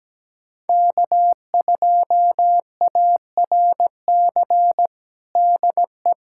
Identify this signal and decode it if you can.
Morse code, heard as a keyed tone, about 17 wpm, 700 Hz. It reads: K2ARC DE
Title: USA Callsign DE Callsign - 5 Characters - 17wpm